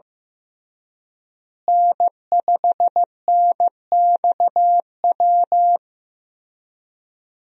Morse code, heard as a keyed tone, about 15 wpm, 700 Hz. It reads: N5NXW
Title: USA Call Signs - 5 Characters - Repeated 2x - 15wpm